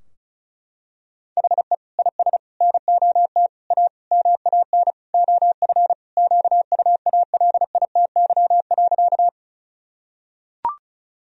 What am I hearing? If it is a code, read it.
Morse code, heard as a keyed tone, about 34 wpm, 700 Hz. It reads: HE IS NOT A MAN OF QUALITY.
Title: Sentences from Top 1000 Words - Review Part 4 - 35wpm